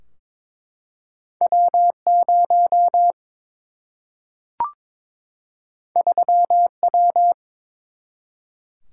Morse code, heard as a keyed tone, about 22 wpm, 700 Hz. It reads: W0 3W
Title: Single-Letter Single-Number 22wpm